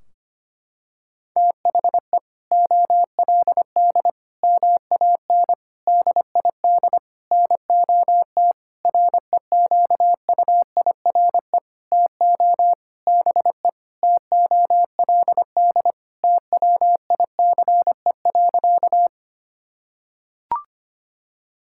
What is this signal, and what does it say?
Morse code, heard as a keyed tone, about 25 wpm, 700 Hz. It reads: THE OLD MAN DID NOT REQUIRE TO BE TOLD TWICE.
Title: Sentences from Top 1600 Words 25wpm